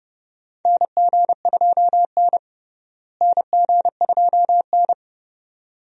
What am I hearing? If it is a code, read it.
Morse code, heard as a keyed tone, about 30 wpm, 700 Hz. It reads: NG2N NG2N
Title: USA Call Signs - 4 Characters - Repeated 2x - 30wpm